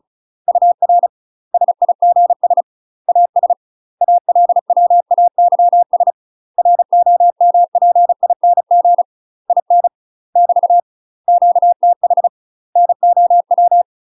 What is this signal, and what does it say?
Morse code, heard as a keyed tone, about 34 wpm, 700 Hz. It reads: UR SIGS AS ALWAYS ROMPING IN <BT> QTH NOW
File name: UK MASTER QSOs TRAINING/35WPM